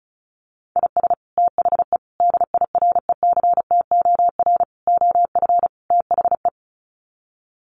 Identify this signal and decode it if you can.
Morse code, heard as a keyed tone, about 34 wpm, 700 Hz. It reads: IS THE DIRECTOR OF THE
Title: Sets of 5 Words Encore2 35wpm